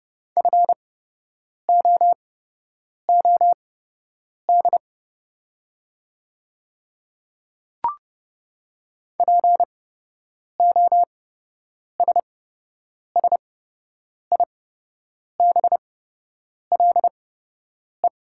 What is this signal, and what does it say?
Morse code, heard as a keyed tone, about 32 wpm, spaced as for 8 wpm, 700 Hz. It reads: FOOD POSSIBLE
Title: 8x Character Spacing - Top 500 Words - 30wpm